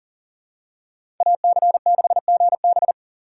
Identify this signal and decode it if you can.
Morse code, heard as a keyed tone, about 40 wpm, 700 Hz. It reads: AC6GB
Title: USA Call Signs - 5 Characters - 3x Character Spacing - 40wpm